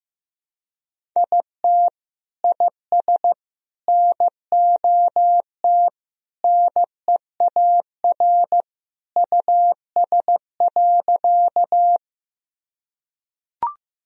Morse code, heard as a keyed tone, about 15 wpm, 700 Hz. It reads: IT IS NOT NEAR US.
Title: Sentences from Top 1000 Words - Review Part 1 - 15wpm